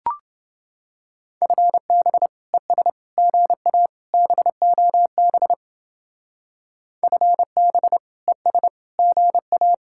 Morse code, heard as a keyed tone, about 29 wpm, 700 Hz. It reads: FB ES GA BOB FB ES GA